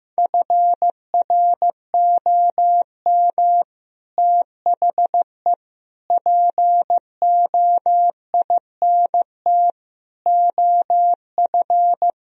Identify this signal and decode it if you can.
Morse code, heard as a keyed tone, about 15 wpm, 700 Hz. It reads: FROM THE POINT OF
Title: Speed-Racing - Sets of 5 Words 15wpm